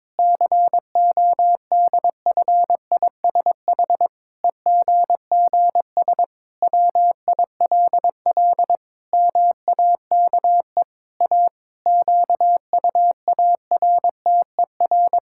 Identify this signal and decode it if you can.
Morse code, heard as a keyed tone, about 22 wpm, 700 Hz. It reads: CODFISH EGGS WILL MAKE A QUARTER
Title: Pangram Challenge v2 - 22wpm